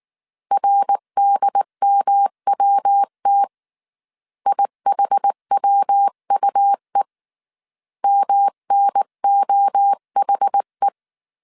Morse code, heard as a keyed tone, about 19 wpm, 800 Hz. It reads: RDMWT IHWUE MNOHE
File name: CW / DET / 1 3 1500C S B 18wpm 1916